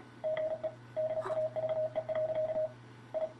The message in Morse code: GE DAVE. I